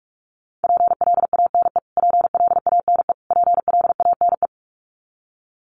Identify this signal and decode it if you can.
Morse code, heard as a keyed tone, about 44 wpm, 700 Hz. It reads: PLANE PLANE PLANE
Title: Top 500 Words Repeated 45wpm